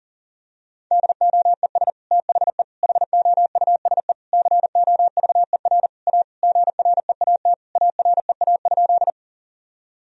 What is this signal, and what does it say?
Morse code, heard as a keyed tone, about 40 wpm, 700 Hz. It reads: DOES THE HOUSE COVER A GREAT AREA?
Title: Sentences from Top 1000 Words - Review Part 2 - 40wpm